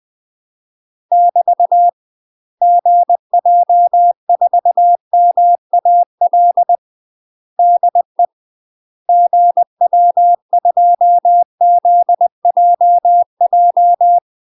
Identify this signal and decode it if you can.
Morse code, heard as a keyed tone, about 20 wpm, 700 Hz. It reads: <BT> GJ4MAL DE GW2ZJJ
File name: UK MASTER QSOs TRAINING/20WPM